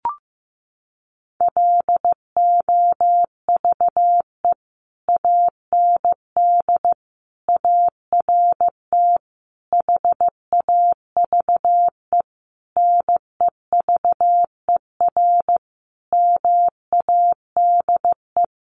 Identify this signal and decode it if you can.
Morse code, heard as a keyed tone, about 15 wpm, 700 Hz. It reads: LOVE AND ART HAVE NEVER MADE
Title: Sentences from Top 1100 Words 15wpm